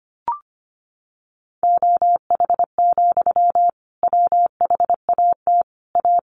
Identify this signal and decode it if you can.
Morse code, heard as a keyed tone, about 25 wpm, 700 Hz. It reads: OH, WHAT A
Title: Sentences from Top 1100 Words 25wpm